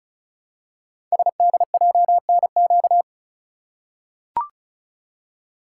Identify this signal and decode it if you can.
Morse code, heard as a keyed tone, about 36 wpm, 700 Hz. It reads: SDJNQ
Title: Code Groups - 5 Letter:Numbers - 35wpm